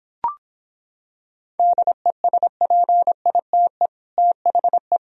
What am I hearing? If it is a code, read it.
Morse code, heard as a keyed tone, about 26 wpm, 700 Hz. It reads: DESPITE THE